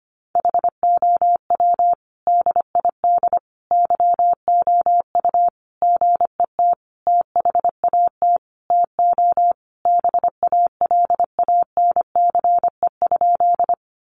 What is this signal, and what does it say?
Morse code, heard as a keyed tone, about 25 wpm, 700 Hz. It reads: HOW DID YOU GET THAT TO BALANCE?
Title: Sentences from Top 900 Words 25wpm